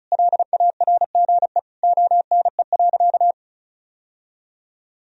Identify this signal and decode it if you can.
Morse code, heard as a keyed tone, about 36 wpm, 700 Hz. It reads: LARGE ONE.
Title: Sentences from Top 1000 Words - Review Part 1 - 35wpm